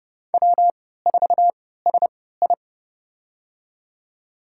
Morse code, heard as a keyed tone, about 29 wpm, spaced as for 17 wpm, 700 Hz. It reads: W4SI
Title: USA Call Signs - 4 Characters - 3x Character Spacing - 30wpm